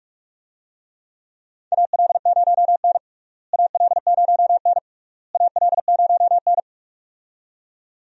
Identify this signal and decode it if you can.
Morse code, heard as a keyed tone, about 46 wpm, 700 Hz. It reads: AL0N AL0N AL0N
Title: USA Call Signs - 4 Characters - Repeated 3x - 45wpm